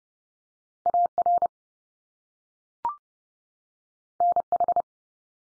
Morse code, heard as a keyed tone, about 30 wpm, 700 Hz. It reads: AR NH
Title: US States 30WPM